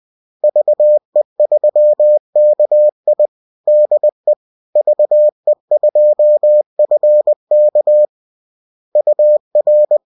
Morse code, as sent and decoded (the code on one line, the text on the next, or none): ...- . ...-- -.- .. / -.. / ...- . ..--- ..-. -.- / ..- .-.
VE3KI DE VE2FK UR